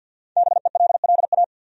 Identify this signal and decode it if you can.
Morse code, heard as a keyed tone, about 51 wpm, 700 Hz. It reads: BELLA
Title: USA Common Names - 50wpm